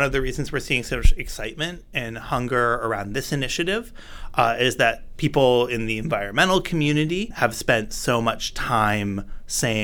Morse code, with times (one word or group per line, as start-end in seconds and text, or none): none